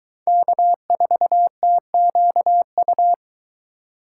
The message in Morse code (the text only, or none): K4TQU